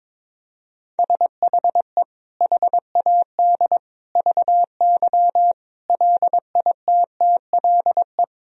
Morse code, heard as a keyed tone, about 22 wpm, 700 Hz. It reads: SHE HAD VY LITTLE
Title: Sentences from Top 400 Words and Abbreviations 22wpm